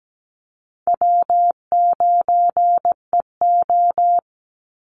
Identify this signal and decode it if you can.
Morse code, heard as a keyed tone, about 17 wpm, 700 Hz. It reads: W9EO
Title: Call Signs Repeated 3x - 17wpm